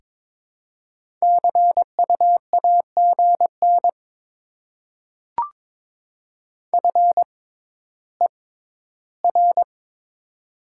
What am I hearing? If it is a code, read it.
Morse code, heard as a keyed tone, about 22 wpm, spaced as for 13 wpm, 700 Hz. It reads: CU AGN FER
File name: Top 20 QSO Elements - 6x Character Spacing - 22wpm